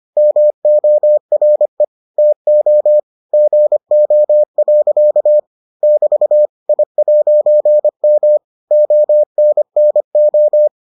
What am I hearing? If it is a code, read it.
Morse code, heard as a keyed tone, about 25 wpm, 600 Hz. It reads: MORE TO GO. <BT> I'M ONNO